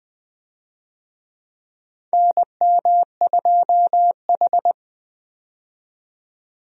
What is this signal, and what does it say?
Morse code, heard as a keyed tone, about 20 wpm, 700 Hz. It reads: NM2H
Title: USA Call Signs - 4 Characters - 20wpm